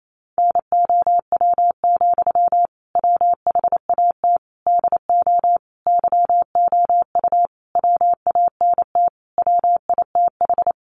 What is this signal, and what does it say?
Morse code, heard as a keyed tone, about 28 wpm, 700 Hz. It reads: NOW, WHAT DO YOU WANT WITH
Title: Sentences from Top 200 Words 28wpm